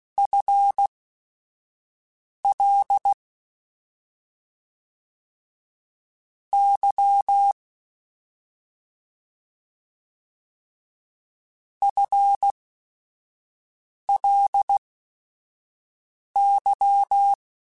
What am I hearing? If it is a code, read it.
Morse code, heard as a keyed tone, about 16 wpm, spaced as for 5 wpm, 800 Hz. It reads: FL Y FLY
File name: FL / FL mp3s / Fast FL @5